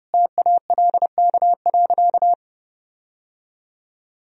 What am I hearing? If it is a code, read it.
Morse code, heard as a keyed tone, about 32 wpm, 700 Hz. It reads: TALK.
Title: Sentences from Top 1000 Words - Review Part 1 - 30wpm